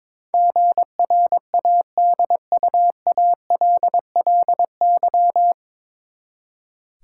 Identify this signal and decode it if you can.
Morse code, heard as a keyed tone, about 22 wpm, 700 Hz. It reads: GRADUALLY
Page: Morse Code Ninja practice 22wpm